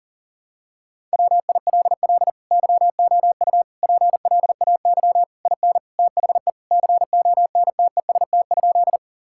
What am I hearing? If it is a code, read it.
Morse code, heard as a keyed tone, about 40 wpm, 700 Hz. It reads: WILL YOU PLAY IN THE CONTEST?
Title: Sentences from Top 1600 Words 40wpm